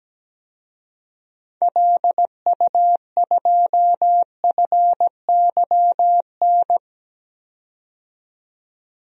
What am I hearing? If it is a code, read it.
Morse code, heard as a keyed tone, about 17 wpm, 700 Hz. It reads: LU2FYN
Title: DXCC-100 Call Signs - 6 Characters - 17wpm